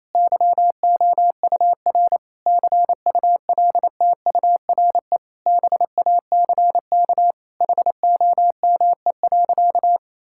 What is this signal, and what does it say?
Morse code, heard as a keyed tone, about 28 wpm, 700 Hz. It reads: YOUR CULTURE BACK HOME.
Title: Sentences from Top 1000 Words - Review Part 6 - 28wpm